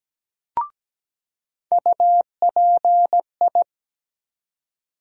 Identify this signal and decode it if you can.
Morse code, heard as a keyed tone, about 17 wpm, 700 Hz. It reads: UPI
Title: USA Call Sign Postfixes - 3 Letters 17wpm